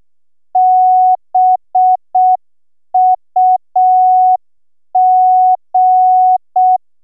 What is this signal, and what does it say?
Morse code, heard as a keyed tone, about 6 wpm, 740 Hz. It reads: BUG